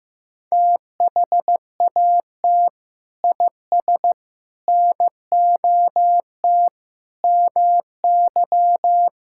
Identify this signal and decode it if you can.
Morse code, heard as a keyed tone, about 15 wpm, 700 Hz. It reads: THAT IS NOT MY
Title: Sentences from Top 1000 Words - Review Part 6 - 15wpm